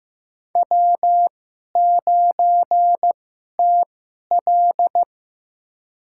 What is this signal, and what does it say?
Morse code, heard as a keyed tone, about 15 wpm, spaced as for 11 wpm, 700 Hz. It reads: W9TL
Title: USA Call Signs - 4 Characters - 2x Character Spacing - 15wpm